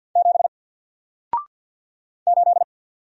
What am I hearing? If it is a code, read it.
Morse code, heard as a keyed tone, about 50 wpm, 700 Hz. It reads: 7 8
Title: Single Letter-Number 50wpm